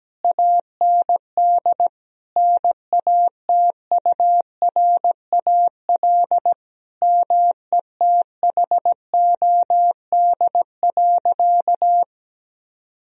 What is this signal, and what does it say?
Morse code, heard as a keyed tone, about 17 wpm, 700 Hz. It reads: AND NATURAL METHOD.
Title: Sentences from Top 1000 Words - Review Part 4 - 17wpm